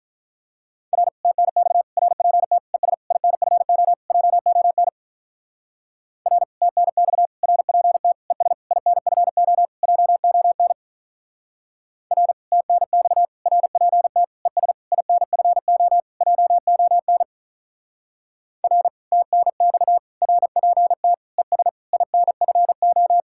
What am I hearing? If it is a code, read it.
Morse code, heard as a keyed tone, about 42 wpm, 700 Hz. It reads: R TNX RPT ES INFO JON R TNX RPT ES INFO JON R TNX RPT ES INFO JON R TNX RPT ES INFO
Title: Speed-Racing Common QSO Phrases Encore - 35wpm